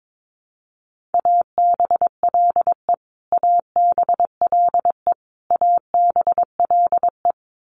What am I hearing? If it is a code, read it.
Morse code, heard as a keyed tone, about 22 wpm, 700 Hz. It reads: ABLE ABLE ABLE
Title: Top 500 Words Repeated 22wpm